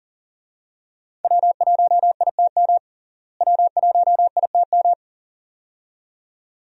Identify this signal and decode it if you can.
Morse code, heard as a keyed tone, about 40 wpm, 700 Hz. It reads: W1ITM W1ITM
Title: USA Call Signs - 5 Characters - Repeated 2x - 40wpm